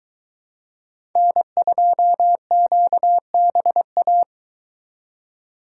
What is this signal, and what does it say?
Morse code, heard as a keyed tone, about 22 wpm, 700 Hz. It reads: N2QBA